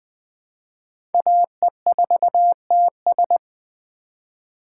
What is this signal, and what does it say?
Morse code, heard as a keyed tone, about 20 wpm, 700 Hz. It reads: AE4TS
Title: USA Call Signs - 3x Character Spacing - 20wpm